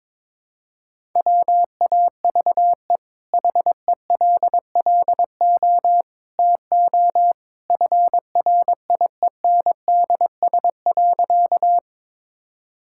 Morse code, heard as a keyed tone, about 22 wpm, 700 Hz. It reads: WAVE HELLO TO FRIENDS.
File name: practice / 22wpm / Easy Sentences with Hints - 22wpm